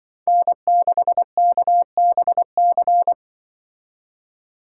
Morse code, heard as a keyed tone, about 25 wpm, 700 Hz. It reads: N6KBC
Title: Speed Racing - USA 1x3 Call Signs 20wpm